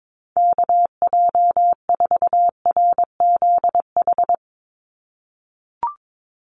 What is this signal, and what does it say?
Morse code, heard as a keyed tone, about 22 wpm, 700 Hz. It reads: KJ4RZH
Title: USA Call Signs 22wpm